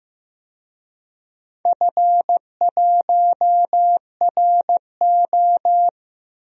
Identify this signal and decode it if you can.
Morse code, heard as a keyed tone, about 15 wpm, 700 Hz. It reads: F1RO
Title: DXCC-100 Call Signs - 4 Characters - 15wpm